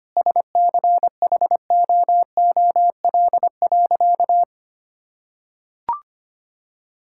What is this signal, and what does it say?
Morse code, heard as a keyed tone, about 25 wpm, 700 Hz. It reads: SCHOOL.
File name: practice / 25wpm / Sentences with 2-Word Phrases - 25wpm